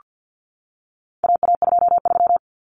Morse code, heard as a keyed tone, about 48 wpm, 700 Hz. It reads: AA23
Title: Common Cut Numbers - 4-Digit 50wpm